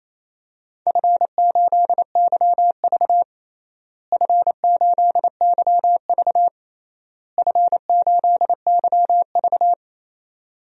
Morse code, heard as a keyed tone, about 27 wpm, 700 Hz. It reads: F8YV F8YV F8YV
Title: DXCC-100 Call Signs - 4 Characters - Repeated 3x - 28wpm